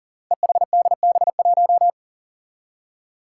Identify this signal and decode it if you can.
Morse code, heard as a keyed tone, about 40 wpm, 700 Hz. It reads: EHDB1
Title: Five Challenging Letter-Number Combinations 40wpm